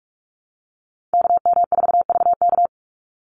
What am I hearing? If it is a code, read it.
Morse code, heard as a keyed tone, about 44 wpm, 700 Hz. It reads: KM4VK
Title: USA Call Signs 45wpm